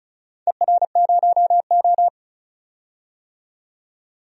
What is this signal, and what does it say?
Morse code, heard as a keyed tone, about 36 wpm, 700 Hz. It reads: ER0O
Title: DXCC-100 Call Signs - 4 Characters - 35wpm